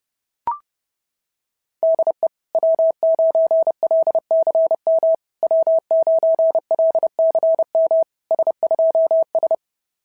Morse code, heard as a keyed tone, about 29 wpm, 660 Hz. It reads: DE W9LCM W9LCM S2S
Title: SOTA QSO Practice 30wpm